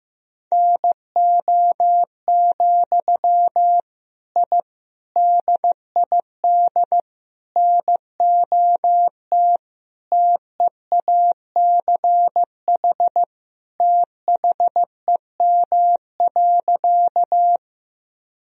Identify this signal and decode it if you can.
Morse code, heard as a keyed tone, about 15 wpm, 700 Hz. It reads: NO, I DID NOT TEACH THEM.
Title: Sentences from Top 1000 Words - Review Part 2 - 15wpm